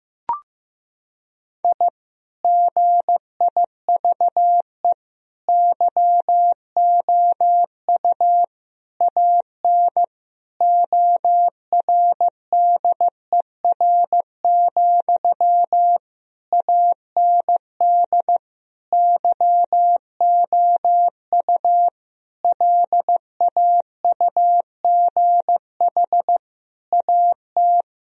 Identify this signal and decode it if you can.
Morse code, heard as a keyed tone, about 15 wpm, 700 Hz. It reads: I GIVE YOU AN ORDER, AND YOU LAUGH AT
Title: Sentences from Top 1000 Words - Review Part 2 - 15wpm